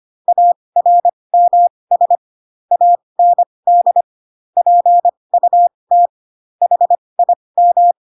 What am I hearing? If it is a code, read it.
Morse code, heard as a keyed tone, about 25 wpm, spaced as for 20 wpm, 700 Hz. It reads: ARMS AND PUT HIM